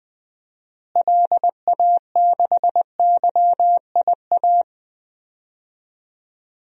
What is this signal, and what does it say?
Morse code, heard as a keyed tone, about 20 wpm, 700 Hz. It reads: LA6YIA
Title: DXCC-100 Call Signs - 6 Characters - Repeated 3x - 20wpm